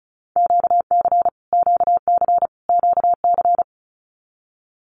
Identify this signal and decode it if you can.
Morse code, heard as a keyed tone, about 34 wpm, 700 Hz. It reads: QC QC QC
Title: Canadian Provinces and Territories - Repeated 3x - 35wpm